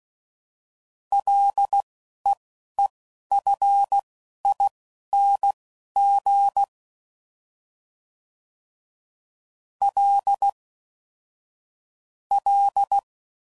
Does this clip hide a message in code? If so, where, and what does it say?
Yes: Morse code from 1.1 to 13.0 s, heard as a keyed tone, about 16 wpm, spaced as for 11 wpm, 800 Hz. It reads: LEEFING L L